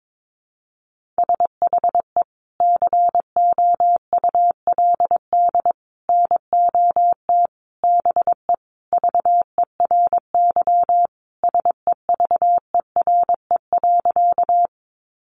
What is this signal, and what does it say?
Morse code, heard as a keyed tone, about 22 wpm, 700 Hz. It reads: SHE COULD NOT BE VERY SEVERE.
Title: Sentences from Top 1400 Words 22wpm